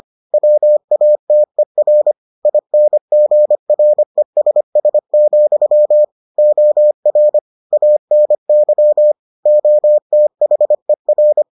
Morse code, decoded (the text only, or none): WATER INGRESS, OR ANY OTHER